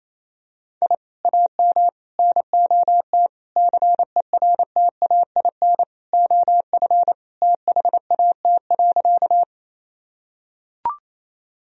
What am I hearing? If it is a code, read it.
Morse code, heard as a keyed tone, about 28 wpm, 700 Hz. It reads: I AM NOT CERTAIN OF THAT.
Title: Sentences from Top 1000 Words - Review Part 2 - 28wpm